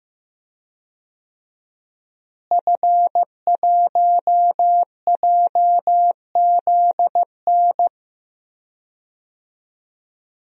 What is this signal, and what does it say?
Morse code, heard as a keyed tone, about 15 wpm, 700 Hz. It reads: F1JZN
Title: DXCC-100 Call Signs - 5 Characters - Repeated 2x - 15wpm